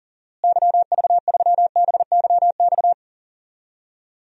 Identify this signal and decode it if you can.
Morse code, heard as a keyed tone, about 37 wpm, 700 Hz. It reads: YV3BYX
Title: DXCC-100 Call Signs - 6 Characters - Repeated 2x - 40wpm